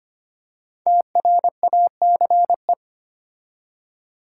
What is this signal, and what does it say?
Morse code, heard as a keyed tone, about 25 wpm, 700 Hz. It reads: TRACE
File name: 5 Letter Words - First 3 Letters Given - 25wpm